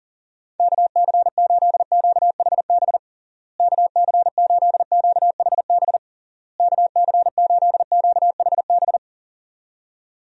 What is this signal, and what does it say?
Morse code, heard as a keyed tone, about 37 wpm, 700 Hz. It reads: KC8QHB KC8QHB KC8QHB
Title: USA Call Signs - 6 Characters - Repeated 3x - 40wpm